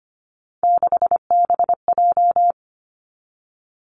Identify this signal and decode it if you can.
Morse code, heard as a keyed tone, about 25 wpm, 700 Hz. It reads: 6BJ
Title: Three Challenging Letter-Number Combinations 25wpm